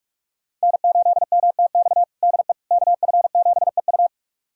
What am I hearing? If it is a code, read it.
Morse code, heard as a keyed tone, about 42 wpm, 700 Hz. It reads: N8MTX DE KF7EU
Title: USA Callsign DE Callsign - 5 Characters - 45wpm